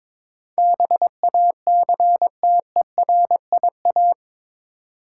Morse code, heard as a keyed tone, about 22 wpm, 700 Hz. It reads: BACTERIA
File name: practice / 22wpm / Eight Letter Words - 22wpm